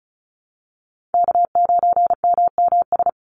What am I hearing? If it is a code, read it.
Morse code, heard as a keyed tone, about 34 wpm, 700 Hz. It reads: K9MMS
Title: Call Signs Repeated 2x - 35wpm